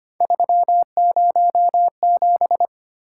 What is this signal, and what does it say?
Morse code, heard as a keyed tone, about 25 wpm, 700 Hz. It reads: 307
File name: Three Letter-Number Combinations - 25wpm